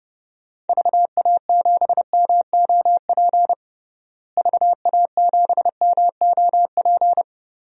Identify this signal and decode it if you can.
Morse code, heard as a keyed tone, about 32 wpm, 700 Hz. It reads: VA7MOP VA7MOP
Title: DXCC-100 Call Signs - 6 Characters - Repeated 3x - 30wpm